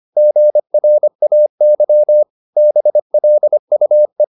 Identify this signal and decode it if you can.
Morse code, heard as a keyed tone, about 25 wpm, 600 Hz. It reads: GRAY BLUE
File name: the old man and the sea part 1 25wpm